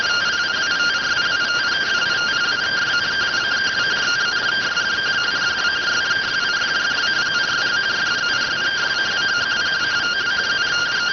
Two-tone teleprinter signal, about 170 Hz shift.